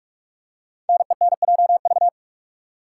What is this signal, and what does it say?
Morse code, heard as a keyed tone, about 44 wpm, 700 Hz. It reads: NENJV
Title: Code Groups - 5 Letters - 45wpm